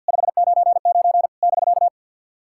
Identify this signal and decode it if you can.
Morse code, heard as a keyed tone, about 49 wpm, 690 Hz. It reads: 599 <BK>
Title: SOTA QSO Practice 50wpm